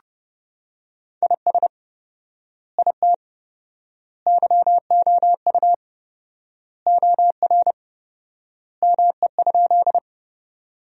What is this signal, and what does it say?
Morse code, heard as a keyed tone, about 30 wpm, 700 Hz. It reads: IS IT YOU OR ME?
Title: 4x Word-Spacing - Sentences from Top 200 Words - 30wpm